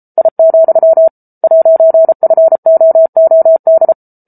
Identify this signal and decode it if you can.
Morse code, heard as a keyed tone, about 34 wpm, 650 Hz. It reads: I, 'FOOD